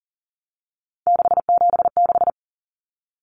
Morse code, heard as a keyed tone, about 40 wpm, 700 Hz. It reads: 676
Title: Numbers - 3-Digit 40wpm